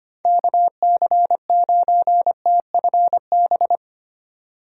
Morse code, heard as a keyed tone, about 30 wpm, 700 Hz. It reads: KC9TFB
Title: Speed Racing - USA Call Signs 25wpm